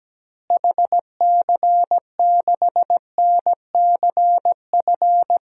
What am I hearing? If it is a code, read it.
Morse code, heard as a keyed tone, about 17 wpm, 700 Hz. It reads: HC6NCF
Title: DXCC-100 Call Signs - 6 Characters - Repeated 3x - 17wpm